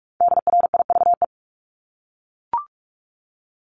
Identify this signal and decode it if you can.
Morse code, heard as a keyed tone, about 44 wpm, 700 Hz. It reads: DRIVE
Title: Top 500 Words Repeated 45wpm